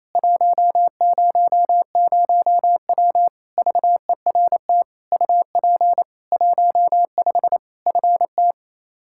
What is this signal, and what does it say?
Morse code, heard as a keyed tone, about 28 wpm, 700 Hz. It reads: 100W VERT UP 15 FT
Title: K1USN SST QSO Practice 28wpm